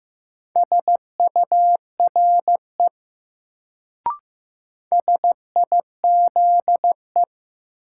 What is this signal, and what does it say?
Morse code, heard as a keyed tone, about 15 wpm, 700 Hz. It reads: SURE SIZE
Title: Sets of 3 Words - New Word Drill - 15wpm